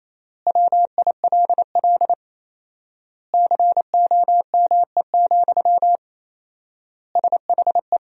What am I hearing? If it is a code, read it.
Morse code, heard as a keyed tone, about 28 wpm, 700 Hz. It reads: WILL COME, SHE
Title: 4x Word-Spacing - Sentences from Top 100 Words - 28wpm